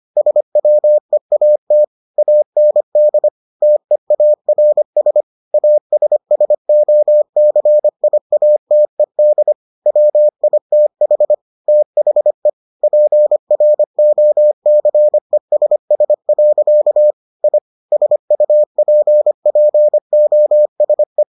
Morse code, heard as a keyed tone, about 25 wpm, 600 Hz. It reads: SWEAT AND TEARS ASSOCIATED WITH THE PROCESS. I SUPPOSE